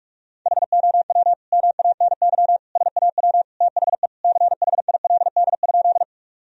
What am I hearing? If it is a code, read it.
Morse code, heard as a keyed tone, about 42 wpm, 700 Hz. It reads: HOW MANY SAW THE CHILD?
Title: Sentences from Top 1000 Words - Review Part 1 - 45wpm